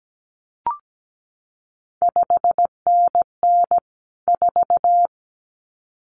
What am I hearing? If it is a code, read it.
Morse code, heard as a keyed tone, about 17 wpm, 700 Hz. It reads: 5NN 4
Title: CQ WW DX Contest Exchange - 17wpm